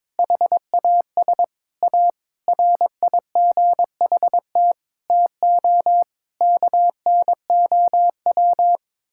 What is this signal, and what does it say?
Morse code, heard as a keyed tone, about 22 wpm, 700 Hz. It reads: HAS A RIGHT TO KNOW
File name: practice / 22wpm / Sets of 7 Words - 22wpm